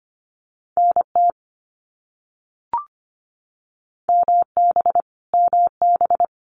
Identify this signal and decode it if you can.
Morse code, heard as a keyed tone, about 25 wpm, 700 Hz. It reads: NT MB MB
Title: Canadian Provinces and Territories - Repeated 3x - 25wpm